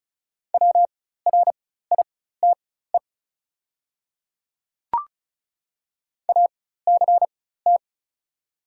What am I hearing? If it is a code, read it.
Morse code, heard as a keyed tone, about 36 wpm, spaced as for 16 wpm, 700 Hz. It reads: WRITE ACT
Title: 4x Character Spacing - Top 200 Words - 35wpm